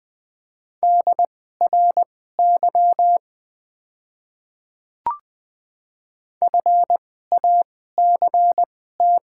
Morse code, heard as a keyed tone, about 20 wpm, spaced as for 15 wpm, 690 Hz. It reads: DRY FACT
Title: 2x Character Spacing - Top 500 Words - 20wpm